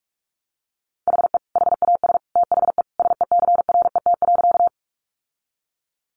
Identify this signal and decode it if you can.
Morse code, heard as a keyed tone, about 44 wpm, 700 Hz. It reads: HE HAS THE SECRET.
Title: Sentences from Top 1200 Words 45wpm